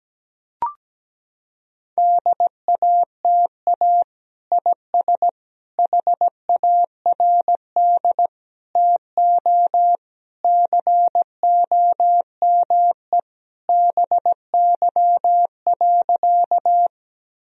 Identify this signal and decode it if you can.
Morse code, heard as a keyed tone, about 17 wpm, 700 Hz. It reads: DATA IS HARD TO COME BY.
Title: Sentences from Top 1000 Words - Review Part 3 - 17wpm